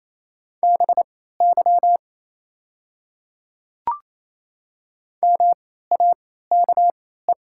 Morse code, heard as a keyed tone, about 28 wpm, spaced as for 16 wpm, 700 Hz. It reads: BY MAKE
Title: 3x Character Spacing - Top 100 Words - 28wpm